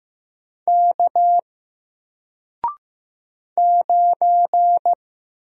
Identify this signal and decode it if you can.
Morse code, heard as a keyed tone, about 15 wpm, 700 Hz. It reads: K 9
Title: Single Character (Letter:Number:Punctuation:Prosign) 15wpm